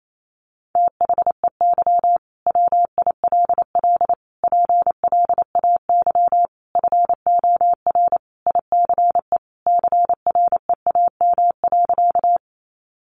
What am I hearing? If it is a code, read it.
Morse code, heard as a keyed tone, about 28 wpm, 700 Hz. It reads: THEY WILL PLAY FOR ICE CREAM.
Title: Sentences from Top 1500 Words 28wpm